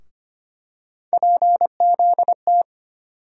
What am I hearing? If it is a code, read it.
Morse code, heard as a keyed tone, about 25 wpm, 700 Hz. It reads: PZT
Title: USA Call Sign Postfixes - 3 Letters 25wpm